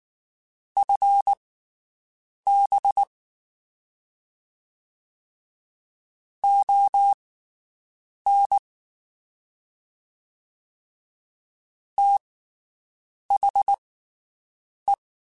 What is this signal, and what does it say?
Morse code, heard as a keyed tone, about 19 wpm, spaced as for 7 wpm, 800 Hz. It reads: FB ON THE
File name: QSOs Spaced